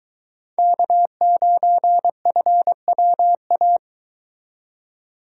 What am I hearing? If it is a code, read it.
Morse code, heard as a keyed tone, about 23 wpm, 700 Hz. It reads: K9FWA